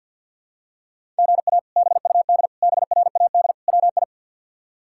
Morse code, heard as a keyed tone, about 51 wpm, 700 Hz. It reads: GA BUD BRAD WI
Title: K1USN SST QSO Practice - 50wpm